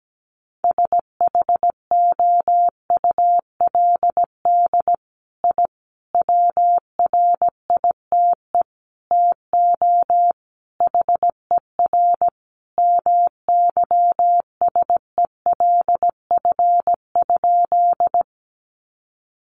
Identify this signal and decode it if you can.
Morse code, heard as a keyed tone, about 17 wpm, 700 Hz. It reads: SHOULD I WRITE TO HER MYSELF?
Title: Sentences from Top 1300 Words 17wpm